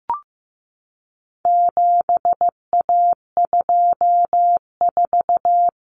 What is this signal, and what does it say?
Morse code, heard as a keyed tone, about 15 wpm, 700 Hz. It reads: 7A24
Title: Common Cut Numbers - 4-Digit 15wpm